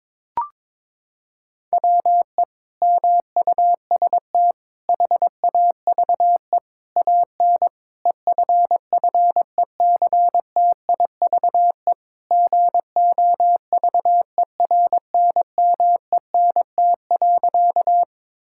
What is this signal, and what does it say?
Morse code, heard as a keyed tone, about 22 wpm, 700 Hz. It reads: WE MUST HAVE AN EFFECTIVE GOVERNMENT.
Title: Sentences from Top 1000 Words - Review Part 6 - 22wpm